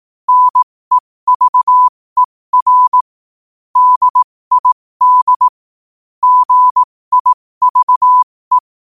Morse code, heard as a keyed tone, about 18 wpm, spaced as for 15 wpm, 1000 Hz. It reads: NEVER DID GIVE
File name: qotdfiles / 15wpm